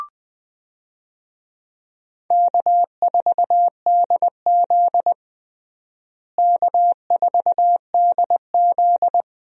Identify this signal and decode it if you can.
Morse code, heard as a keyed tone, about 20 wpm, 700 Hz. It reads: K4DZ K4DZ
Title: USA Call Signs - 4 Characters - Repeated 2x - 20wpm